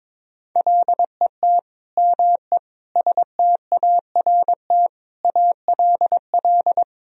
Morse code, heard as a keyed tone, about 22 wpm, 700 Hz. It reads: LET ME START ALL
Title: Sentences from Top 1000 Words - Review Part 1 - 22wpm